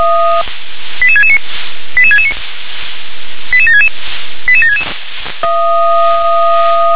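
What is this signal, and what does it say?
Five-tone selective call (ZVEI), heard as quick runs of five tones, tones of about 70 ms. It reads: 88669 88669 88766 88766
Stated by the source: Five-tone ZVEI selective call received on an SDR.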